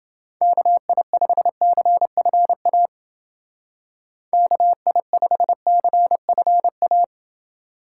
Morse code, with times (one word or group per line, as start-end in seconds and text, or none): 0.4-2.9 KI5CFA
4.3-7.0 KI5CFA